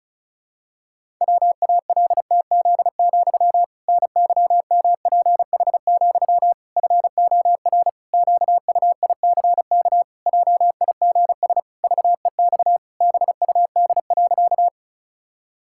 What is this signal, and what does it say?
Morse code, heard as a keyed tone, about 36 wpm, 700 Hz. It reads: WALTZ, NYMPH, FOR QUICK JIGS VEX BUD.
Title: Pangram Challenge v2 - 35wpm